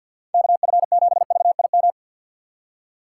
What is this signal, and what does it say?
Morse code, heard as a keyed tone, about 48 wpm, 700 Hz. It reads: KF7VIM
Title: USA Call Signs - 6 Characters - 6x Character Spacing - 50wpm